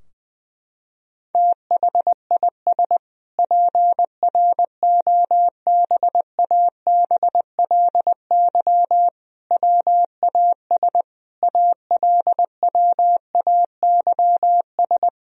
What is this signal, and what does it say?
Morse code, heard as a keyed tone, about 21 wpm, 700 Hz. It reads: THIS PROBABLY WAS ALWAYS
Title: Sentences from Top 1000 Words - Review Part 4 - 20wpm